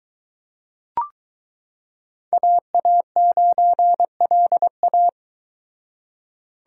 Morse code, heard as a keyed tone, about 22 wpm, 700 Hz. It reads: AA9LA